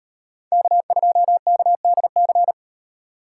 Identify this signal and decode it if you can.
Morse code, heard as a keyed tone, about 36 wpm, 700 Hz. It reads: K2KDC